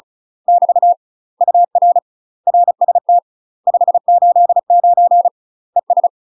Morse code, heard as a keyed tone, about 36 wpm, 700 Hz. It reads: <BT> UR RST 589 ES
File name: UK MASTER QSOs TRAINING / 35WPM / UK MASTER QSOs 0018